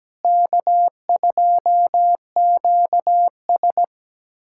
Morse code, heard as a keyed tone, about 17 wpm, 700 Hz. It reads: K2QS
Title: USA Callsign DE Callsign - 4 Characters - 17wpm